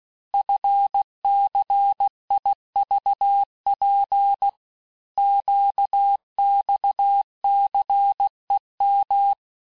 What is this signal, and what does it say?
Morse code, heard as a keyed tone, about 16 wpm, 800 Hz. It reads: FCIVP QXCEM
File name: cours-du-lundi files / cours-1592--teste-chiffre-vitesse-15mpm